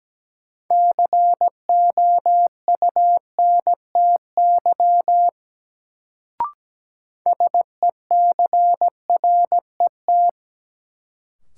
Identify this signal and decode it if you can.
Morse code, heard as a keyed tone, about 17 wpm, 700 Hz. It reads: COUNTY SECRET
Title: Sentences from Top 1200 Words - New Word Drill - 17wpm